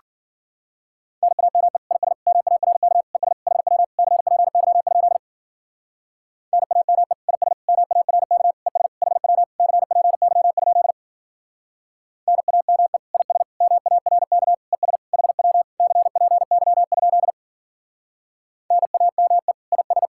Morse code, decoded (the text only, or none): NAME IS MARK ES HW CPY? NAME IS MARK ES HW CPY? NAME IS MARK ES HW CPY? NAME IS